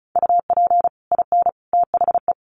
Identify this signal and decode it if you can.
Morse code, heard as a keyed tone, about 34 wpm, 700 Hz. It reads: UP IN THE
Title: Sets of 3 Words 35wpm